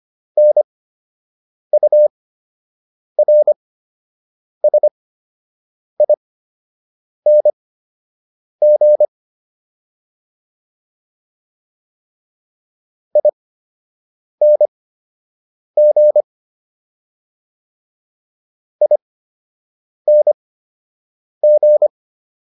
Morse code, heard as a keyed tone, about 25 wpm, spaced as for 7 wpm, 600 Hz. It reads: NURSING ING ING